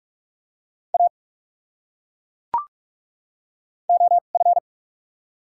Morse code, heard as a keyed tone, about 42 wpm, spaced as for 31 wpm, 700 Hz. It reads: A OF
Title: 2x Character Spacing - Top 200 Words - 45wpm